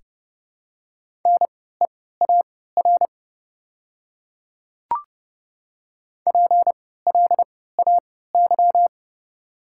Morse code, heard as a keyed tone, about 30 wpm, spaced as for 17 wpm, 700 Hz. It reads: NEAR PLAY
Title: 3x Character Spacing - Top 200 Words - 30wpm